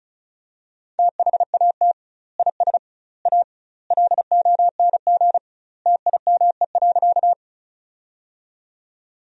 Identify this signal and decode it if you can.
Morse code, heard as a keyed tone, about 36 wpm, 700 Hz. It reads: THAT IS A LONG TIME.
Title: 2x Word-Spacing - Sentences from Top 100 Words - 35wpm